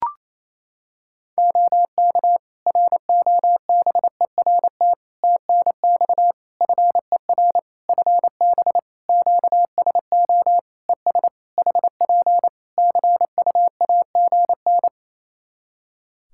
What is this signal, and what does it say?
Morse code, heard as a keyed tone, about 27 wpm, 700 Hz. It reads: OK ROBERT TNX FER FB QSO ES HP CUAGN
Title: K1USN SST QSO Practice 28wpm